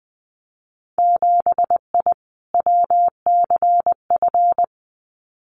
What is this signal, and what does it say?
Morse code, heard as a keyed tone, about 20 wpm, 700 Hz. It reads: 7I WCF
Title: Winter Field Day Exchanges Encore 20wpm